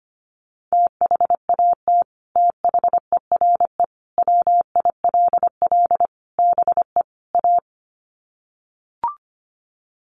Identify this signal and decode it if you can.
Morse code, heard as a keyed tone, about 25 wpm, 700 Hz. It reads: THAT THERE WILL BE A
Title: Sets of 5 Words Encore2 25wpm